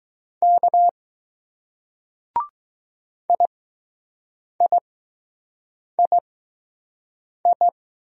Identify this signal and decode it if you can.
Morse code, heard as a keyed tone, about 20 wpm, spaced as for 6 wpm, 700 Hz. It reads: K IIII